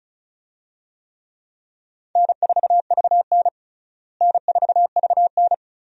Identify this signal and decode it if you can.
Morse code, heard as a keyed tone, about 34 wpm, 700 Hz. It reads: N4VN N4VN
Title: USA Call Signs - 4 Characters - Repeated 2x - 35wpm